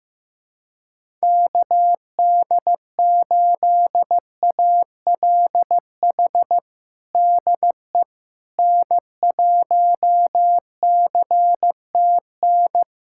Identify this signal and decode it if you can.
Morse code, heard as a keyed tone, about 15 wpm, 700 Hz. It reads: KD8ALH DE N1CTN
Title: USA Callsign DE Callsign 15wpm